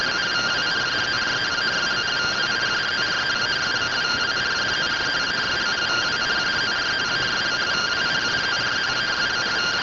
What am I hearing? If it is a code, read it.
Two-tone teleprinter signal, about 170 Hz shift.